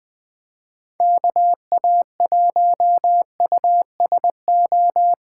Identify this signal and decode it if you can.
Morse code, heard as a keyed tone, about 23 wpm, 700 Hz. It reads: KA1USO
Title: Speed Racing - USA Call Signs 20wpm